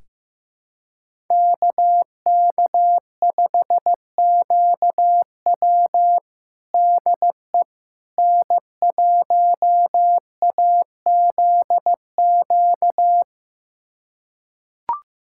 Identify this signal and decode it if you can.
Morse code, heard as a keyed tone, about 15 wpm, 700 Hz. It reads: KK5QW DE N1AZQ
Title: USA Callsign DE Callsign - 5 Characters - 15wpm